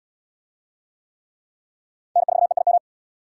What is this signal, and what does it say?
Morse code, heard as a keyed tone, about 77 wpm, 700 Hz. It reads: N4EIM